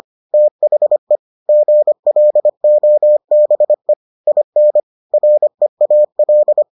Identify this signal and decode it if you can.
Morse code, heard as a keyed tone, about 25 wpm, 600 Hz. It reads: THE GLOBE IN REAL